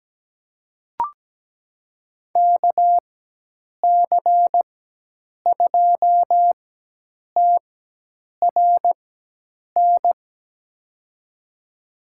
Morse code, heard as a keyed tone, about 17 wpm, spaced as for 8 wpm, 700 Hz. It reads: KC2TRN
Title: USA Call Signs - 6 Characters - 4x Character Spacing - 17wpm